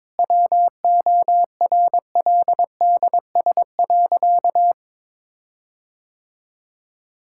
Morse code, heard as a keyed tone, about 22 wpm, 700 Hz. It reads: WORLDS.
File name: Difficult Sentences with Hints - 22wpm